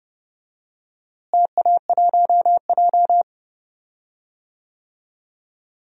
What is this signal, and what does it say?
Morse code, heard as a keyed tone, about 30 wpm, 700 Hz. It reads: TA1J
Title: DXCC-100 Call Signs - 4 Characters - Repeated 3x - 30wpm